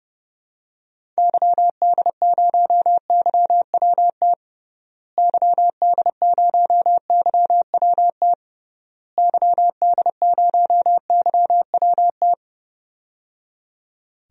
Morse code, heard as a keyed tone, about 30 wpm, 700 Hz. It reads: YD0YWT YD0YWT YD0YWT
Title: DXCC-100 Call Signs - 6 Characters - Repeated 3x - 30wpm